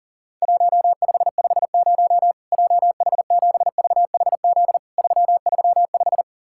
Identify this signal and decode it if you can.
Morse code, heard as a keyed tone, about 40 wpm, 700 Hz. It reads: 1550 JH7VHZ 335
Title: Worked All Europe DX Contest - QTC Exchange - 40wpm